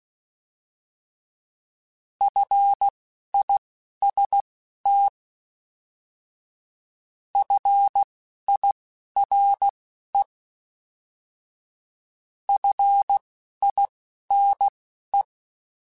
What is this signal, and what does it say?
Morse code, heard as a keyed tone, about 16 wpm, spaced as for 11 wpm, 800 Hz. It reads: FIST FIRE FINE
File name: Alphabetical / Alphabetical mp3s / Fs @10